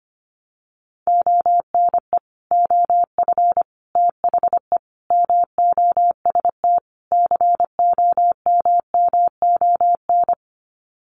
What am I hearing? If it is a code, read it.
Morse code, heard as a keyed tone, about 25 wpm, 700 Hz. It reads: ONE OF THE MOST COMMON
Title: Sets of 5 Words Encore2 25wpm